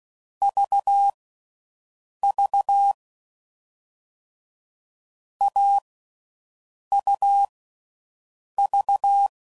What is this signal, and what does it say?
Morse code, heard as a keyed tone, about 16 wpm, spaced as for 6 wpm, 800 Hz. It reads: VV AUV
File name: Combinations / Difficult Combos / AUV ShortRun